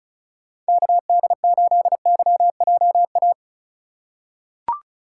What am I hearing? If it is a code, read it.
Morse code, heard as a keyed tone, about 36 wpm, 700 Hz. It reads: KD8YJA
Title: USA Call Signs - 2x Character Spacing - 35wpm